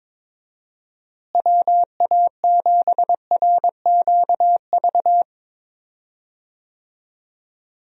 Morse code, heard as a keyed tone, about 22 wpm, 700 Hz. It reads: WA7RQV
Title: USA Call Signs - 6 Characters - Repeated 3x - 22wpm